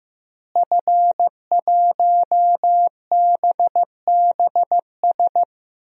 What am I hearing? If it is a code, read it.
Morse code, heard as a keyed tone, about 15 wpm, 700 Hz. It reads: F1BBS
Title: DXCC-100 Call Signs - 5 Characters - Repeated 3x - 15wpm